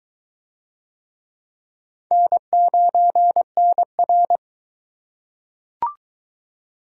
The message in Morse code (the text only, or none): N9NR